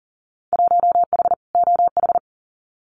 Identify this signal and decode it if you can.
Morse code, heard as a keyed tone, about 40 wpm, 700 Hz. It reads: 1H OH
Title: Winter Field Day Exchanges 40wpm